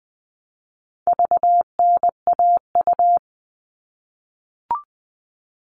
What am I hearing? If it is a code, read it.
Morse code, heard as a keyed tone, about 20 wpm, 700 Hz. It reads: VNAU
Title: All Cut Numbers - 4-Digit 20wpm